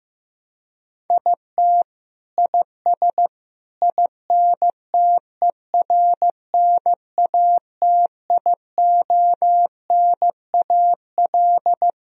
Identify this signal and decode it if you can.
Morse code, heard as a keyed tone, about 15 wpm, 700 Hz. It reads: IT IS INTERNATIONAL
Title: Sentences from Top 1000 Words - Review Part 4 - 15wpm